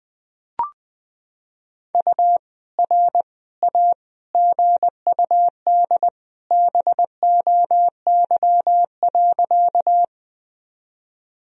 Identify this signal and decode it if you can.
Morse code, heard as a keyed tone, about 20 wpm, 700 Hz. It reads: U R A GUD BOY.
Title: Sentences from Top 200 Words and Abbreviations 20wpm